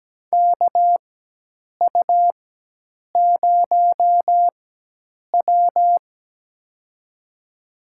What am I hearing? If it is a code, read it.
Morse code, heard as a keyed tone, about 17 wpm, spaced as for 8 wpm, 700 Hz. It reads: KU0W
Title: USA Call Signs - 4 Characters - 4x Character Spacing - 17wpm